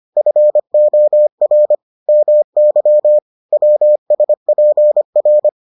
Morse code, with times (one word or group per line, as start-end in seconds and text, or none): 0.2-1.8 FOR
2.1-3.2 MY
3.5-5.5 WSPR